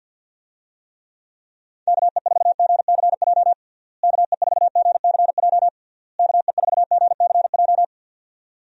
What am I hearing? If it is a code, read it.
Morse code, heard as a keyed tone, about 51 wpm, 700 Hz. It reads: KE4GCJ KE4GCJ KE4GCJ
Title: USA Call Signs - 6 Characters - Repeated 3x - 50wpm